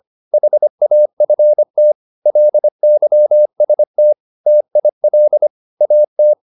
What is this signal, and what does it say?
Morse code, heard as a keyed tone, about 25 wpm, 600 Hz. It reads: HAFT LYST TIL AT
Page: Ferie 25 wpm